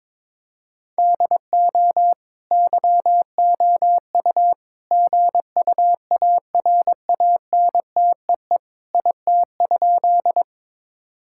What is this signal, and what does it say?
Morse code, heard as a keyed tone, about 22 wpm, 700 Hz. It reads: DO YOU GUARANTEE IT?
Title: Sentences from Top 1600 Words 22wpm